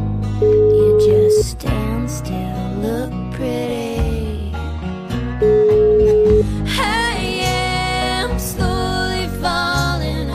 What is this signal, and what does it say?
Ringback tone (Europe, 425 Hz), with noise.